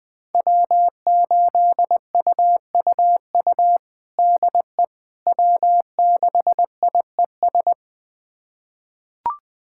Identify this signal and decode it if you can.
Morse code, heard as a keyed tone, about 20 wpm, 700 Hz. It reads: W8UUU DE W6IES
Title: USA Callsign DE Callsign - 5 Characters - 20wpm